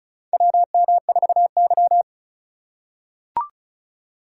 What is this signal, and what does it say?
Morse code, heard as a keyed tone, about 36 wpm, 700 Hz. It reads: WM4Y
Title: USA Call Signs - 4 Characters - 2x Character Spacing - 35wpm